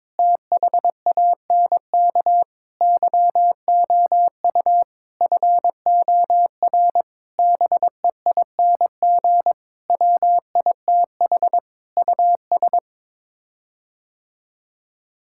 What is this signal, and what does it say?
Morse code, heard as a keyed tone, about 22 wpm, 700 Hz. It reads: THANK YOU FOR BEING WITH US
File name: Sets of 9 Words - 22wpm